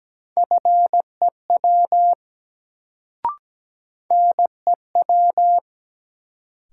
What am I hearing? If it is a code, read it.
Morse code, heard as a keyed tone, about 17 wpm, 700 Hz. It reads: FEW NEW
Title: Sentences from Top 300 Words - New Word Drill - 17wpm